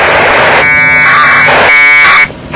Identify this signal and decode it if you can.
Packet radio (AX.25, as in APRS).
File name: vk3ukf / ISS packet QSO